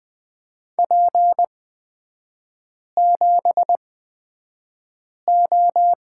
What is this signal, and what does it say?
Morse code, heard as a keyed tone, about 20 wpm, spaced as for 5 wpm, 700 Hz. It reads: P7O
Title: 243 Lesson 39e - Letter z - Three Characters 20wpm